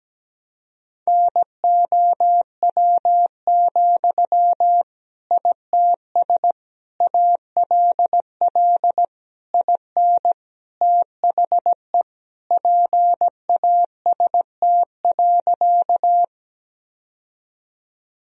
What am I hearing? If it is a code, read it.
Morse code, heard as a keyed tone, about 17 wpm, 700 Hz. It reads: NOW, ITS ALL IN THE PAST.
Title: Sentences from Top 1000 Words - Review Part 3 - 17wpm